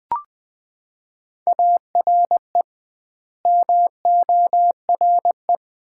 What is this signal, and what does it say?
Morse code, heard as a keyed tone, about 20 wpm, 700 Hz. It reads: ARE MORE
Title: 2x Word-Spacing - Sets of 4 Words - 20wpm